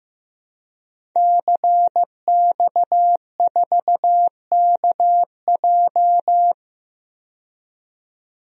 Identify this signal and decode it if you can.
Morse code, heard as a keyed tone, about 15 wpm, 700 Hz. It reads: CX4KJ
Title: DXCC-100 Call Signs - 5 Characters - 15wpm